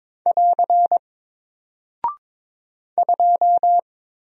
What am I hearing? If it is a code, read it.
Morse code, heard as a keyed tone, about 22 wpm, 700 Hz. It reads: <AR> 2
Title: Single Character (Letter:Number:Punctuation:Prosign) 22wpm